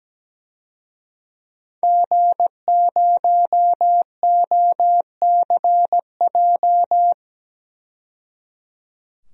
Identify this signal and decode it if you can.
Morse code, heard as a keyed tone, about 17 wpm, 700 Hz. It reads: G0OCJ
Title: DXCC-100 Call Signs - 5 Characters - 17wpm